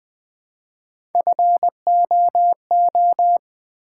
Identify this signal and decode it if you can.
Morse code, heard as a keyed tone, about 20 wpm, 700 Hz. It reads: FOO
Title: Build A Word - 1 Letter At A Time - 3 to 5-Letter Words - 20wpm